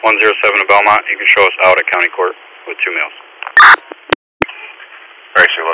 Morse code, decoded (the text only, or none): none